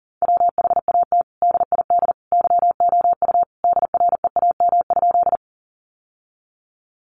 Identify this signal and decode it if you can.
Morse code, heard as a keyed tone, about 40 wpm, 700 Hz. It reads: WHAT DID YOU DREAM?
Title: Sentences from Top 1500 Words 40wpm